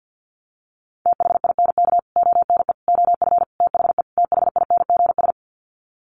Morse code, heard as a keyed tone, about 50 wpm, 700 Hz. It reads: THINK ONE OF THE THINGS
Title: Sets of 5 Words Encore2 50wpm